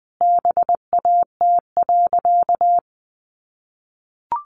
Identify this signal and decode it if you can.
Morse code, heard as a keyed tone, about 20 wpm, 700 Hz. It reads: BAT.
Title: Sentences from Top 1500 Words 20wpm